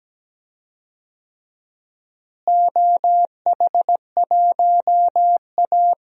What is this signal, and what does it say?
Morse code, heard as a keyed tone, about 17 wpm, 700 Hz. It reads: OH1A
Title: DXCC-100 Call Signs - 4 Characters - Repeated 3x - 17wpm